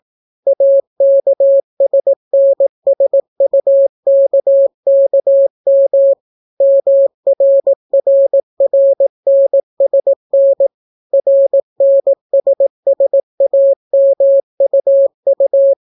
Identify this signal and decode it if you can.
Morse code, heard as a keyed tone, about 18 wpm, 550 Hz. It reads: AKSNSUKKM MRRRNSN RNSSAMUU